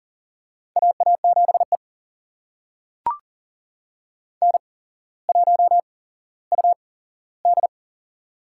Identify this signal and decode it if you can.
Morse code, heard as a keyed tone, about 38 wpm, spaced as for 17 wpm, 700 Hz. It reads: AA7E N1UD
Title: USA Call Signs - 4 Characters - 8x Character Spacing - 40wpm